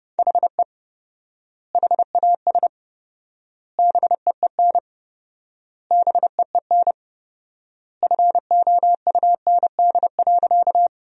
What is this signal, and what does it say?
Morse code, heard as a keyed tone, about 32 wpm, 700 Hz. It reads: HE HAS BEEN BEEN FOUND.
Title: 4x Word-Spacing - Sentences from Top 300 Words - 30wpm